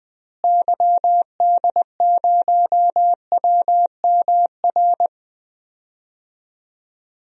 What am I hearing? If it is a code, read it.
Morse code, heard as a keyed tone, about 21 wpm, 700 Hz. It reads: YD0WMR
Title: DXCC-100 Call Signs - 6 Characters - Repeated 3x - 20wpm